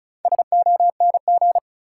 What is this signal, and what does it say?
Morse code, heard as a keyed tone, about 36 wpm, 700 Hz. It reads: SONG
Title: Sentences from Top 400 Words - New Word Drill - 35wpm